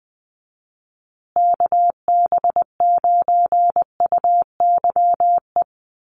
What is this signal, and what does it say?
Morse code, heard as a keyed tone, about 20 wpm, 700 Hz. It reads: KB9UYE
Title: USA Call Signs Repeated 3x 20wpm